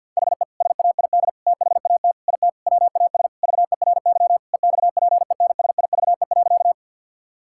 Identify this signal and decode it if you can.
Morse code, heard as a keyed tone, about 50 wpm, 700 Hz. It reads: HE SAID THAT IT WAS VERY EXPENSIVE.
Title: Sentences from Top 1000 Words - Review Part 6 - 50wpm